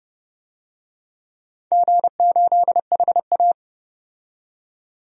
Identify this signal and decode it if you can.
Morse code, heard as a keyed tone, about 30 wpm, 700 Hz. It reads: G8HA
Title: DXCC-100 Call Signs - 4 Characters - 30wpm